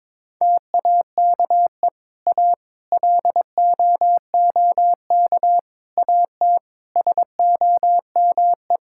Morse code, heard as a keyed tone, about 22 wpm, 700 Hz. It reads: TAKE A LOOK AT SOME
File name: practice / 22wpm / Sets of 7 Words - 22wpm